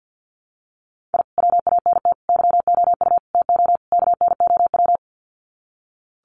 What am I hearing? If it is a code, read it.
Morse code, heard as a keyed tone, about 50 wpm, 700 Hz. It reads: I WANT YOU TO KNOW
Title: Sets of 5 Words Encore 50wpm